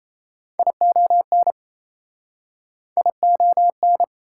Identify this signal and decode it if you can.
Morse code, heard as a keyed tone, about 41 wpm, 690 Hz. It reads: ION ION